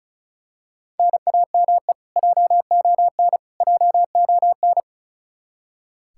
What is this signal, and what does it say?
Morse code, heard as a keyed tone, about 34 wpm, 700 Hz. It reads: NAME JON JON
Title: K1USN SST QSO Practice 35wpm